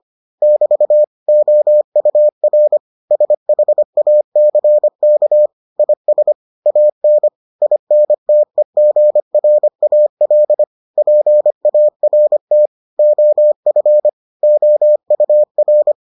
Morse code, heard as a keyed tone, about 25 wpm, 600 Hz. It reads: <BT> OUR SHACK IS AN INTEGRAL PART OF OUR